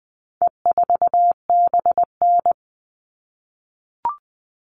Morse code, heard as a keyed tone, about 20 wpm, 700 Hz. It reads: E4BN
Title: All Cut Numbers - 4-Digit 20wpm